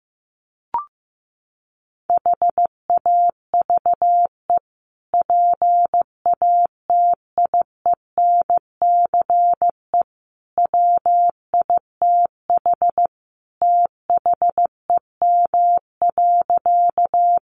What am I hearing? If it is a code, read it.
Morse code, heard as a keyed tone, about 15 wpm, 700 Hz. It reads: HAVE PATIENCE WITH THEM.
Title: Sentences from Top 1300 Words 15wpm